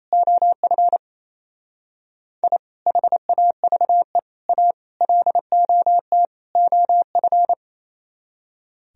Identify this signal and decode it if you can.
Morse code, heard as a keyed tone, about 34 wpm, 700 Hz. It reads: OF I HAVE A LOT OF
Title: Speed-Racing - Sets of 5 Words 28wpm